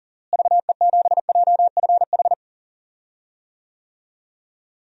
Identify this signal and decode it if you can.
Morse code, heard as a keyed tone, about 40 wpm, 700 Hz. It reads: VE7JFH
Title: DXCC-100 Call Signs - 6 Characters - Repeated 2x - 40wpm